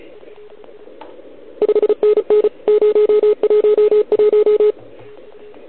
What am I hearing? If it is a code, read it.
Morse code, heard as a keyed tone, about 35 wpm, 410 Hz. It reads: 5NN 011
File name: ARCHIV / OKDX2020 / OL4Wa